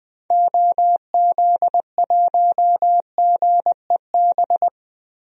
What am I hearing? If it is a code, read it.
Morse code, heard as a keyed tone, about 20 wpm, 700 Hz. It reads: OZ1GEB
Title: DXCC-100 Call Signs - 6 Characters - 20wpm